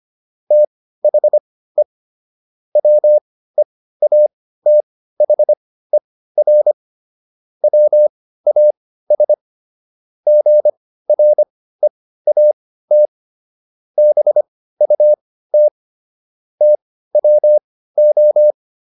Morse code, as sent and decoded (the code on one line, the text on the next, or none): - .... / .-- . .- - .... .-. / .-- .- ... / --. .-. .- - / -... ..- - / - .-- ---
THE WEATHER WAS GREAT BUT TWO